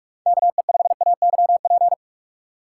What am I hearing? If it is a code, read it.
Morse code, heard as a keyed tone, about 41 wpm, 700 Hz. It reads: KE5AYP
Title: USA Call Signs - 6 Characters - 3x Character Spacing - 45wpm